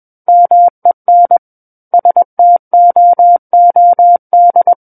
Morse code, heard as a keyed tone, about 21 wpm, 700 Hz. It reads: MEN STOOD
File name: Book One / 12-Chapter 12 - 21 WPM